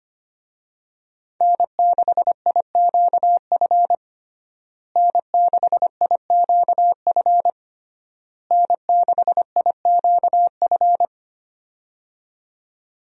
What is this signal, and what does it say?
Morse code, heard as a keyed tone, about 25 wpm, 700 Hz. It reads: N6IQF N6IQF N6IQF
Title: USA Call Signs - 5 Characters - Repeated 3x - 25wpm